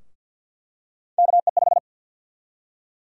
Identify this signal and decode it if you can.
Morse code, heard as a keyed tone, about 50 wpm, 700 Hz. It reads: KE5
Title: USA Call Sign Prefixes - 50wpm